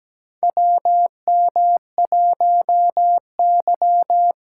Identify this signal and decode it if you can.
Morse code, heard as a keyed tone, about 17 wpm, 700 Hz. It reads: WM1Y
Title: USA Callsign DE Callsign - 4 Characters - 17wpm